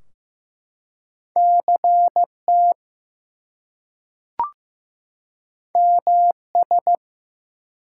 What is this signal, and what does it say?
Morse code, heard as a keyed tone, about 15 wpm, 700 Hz. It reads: CT MS
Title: US States and Canadian Provinces 15wpm